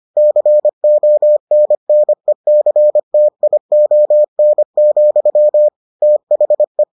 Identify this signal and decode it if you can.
Morse code, heard as a keyed tone, about 25 wpm, 600 Hz. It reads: CONNECTION, THE